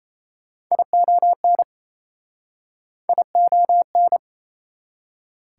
Morse code, heard as a keyed tone, about 37 wpm, 700 Hz. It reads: ION ION